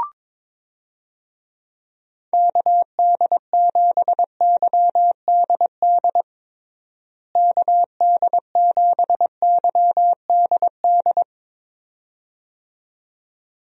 Morse code, heard as a keyed tone, about 22 wpm, 700 Hz. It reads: KD7YDD KD7YDD
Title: USA Call Signs - 6 Characters - Repeated 2x - 22wpm